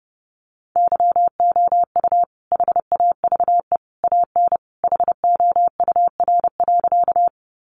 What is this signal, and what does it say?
Morse code, heard as a keyed tone, about 30 wpm, 700 Hz. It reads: YOU HAVE AN HOUR.
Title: Sentences from Top 400 Words 30wpm